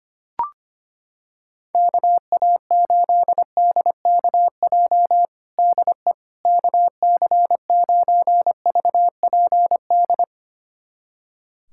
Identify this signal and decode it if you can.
Morse code, heard as a keyed tone, about 25 wpm, 700 Hz. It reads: KA8DKJ DE KC9VPD
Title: USA Callsign DE Callsign - 6 Characters - 25wpm